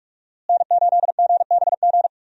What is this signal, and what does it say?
Morse code, heard as a keyed tone, about 43 wpm, 700 Hz. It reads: N8GBG
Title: USA Call Signs - 5 Characters - Repeated 3x - 45wpm